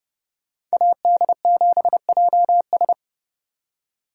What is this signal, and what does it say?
Morse code, heard as a keyed tone, about 36 wpm, 700 Hz. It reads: AD7JS
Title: Speed Racing - USA 2x2 Call Signs 30wpm